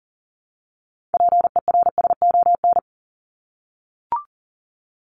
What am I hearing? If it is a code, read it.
Morse code, heard as a keyed tone, about 40 wpm, 700 Hz. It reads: PERSON
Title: Top 500 Words Repeated 40wpm